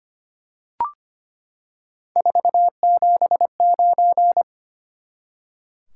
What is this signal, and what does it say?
Morse code, heard as a keyed tone, about 25 wpm, 700 Hz. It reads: 479
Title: Top 180 QSO Elements 25wpm